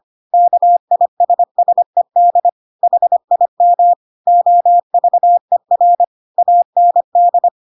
Morse code, heard as a keyed tone, about 25 wpm, 700 Hz. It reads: KISSED HIM OVER AND